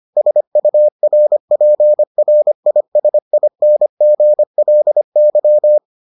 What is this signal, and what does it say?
Morse code, heard as a keyed tone, about 25 wpm, 600 Hz. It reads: SURPRISINGLY